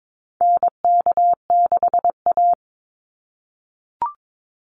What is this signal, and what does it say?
Morse code, heard as a keyed tone, about 22 wpm, 700 Hz. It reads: NK6A
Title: NA SOTA Chaser Callsigns 22wpm